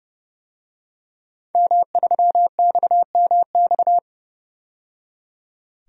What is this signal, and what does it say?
Morse code, heard as a keyed tone, about 32 wpm, 700 Hz. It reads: M3XMX
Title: DXCC-100 Call Signs - 5 Characters - 30wpm